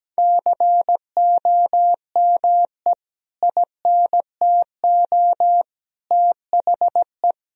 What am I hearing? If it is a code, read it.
Morse code, heard as a keyed tone, about 17 wpm, 700 Hz. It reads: COME INTO THE
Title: Sentences from Top 1000 Words - Review Part 2 - 17wpm